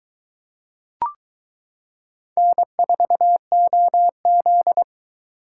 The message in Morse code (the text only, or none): N4OZ